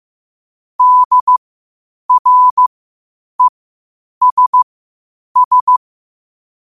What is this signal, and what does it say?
Morse code, heard as a keyed tone, about 15 wpm, spaced as for 8 wpm, 1000 Hz. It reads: DRESS
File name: qotdfiles / 8wpm